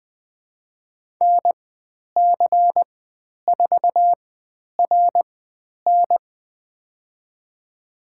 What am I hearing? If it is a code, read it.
Morse code, heard as a keyed tone, about 20 wpm, spaced as for 10 wpm, 700 Hz. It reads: NC4RN
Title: 109 Lesson 18g - Letter m - Call Signs 20wpm